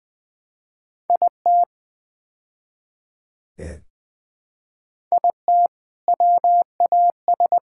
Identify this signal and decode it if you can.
Morse code, heard as a keyed tone, about 20 wpm, 700 Hz. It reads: IT IT WAS
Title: Build A Phrase - 5-Word Phrases - 20wpm